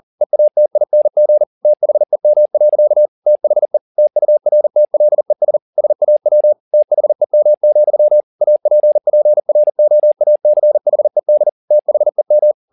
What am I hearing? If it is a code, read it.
Morse code, heard as a keyed tone, about 40 wpm, 600 Hz. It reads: EATING THEM. THE TURTLES SAW THEM, APPROACHED THEM